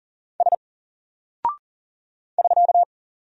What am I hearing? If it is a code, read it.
Morse code, heard as a keyed tone, about 40 wpm, 700 Hz. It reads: S <SK>
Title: Single Character (Letter:Number:Punctuation:Prosign) 40wpm